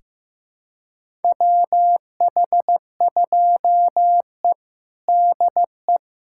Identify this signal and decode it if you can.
Morse code, heard as a keyed tone, about 15 wpm, 700 Hz. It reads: WH2E DE
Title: USA Callsign DE Callsign - 4 Characters - 15wpm